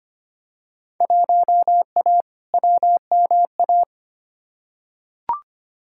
Morse code, heard as a keyed tone, about 25 wpm, 700 Hz. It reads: 1A WMA
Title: Field Day Exchanges - 25wpm